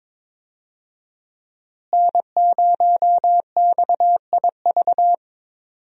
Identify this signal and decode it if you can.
Morse code, heard as a keyed tone, about 22 wpm, 700 Hz. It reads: N0XIV
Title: USA Call Signs - 5 Characters - Repeated 3x - 22wpm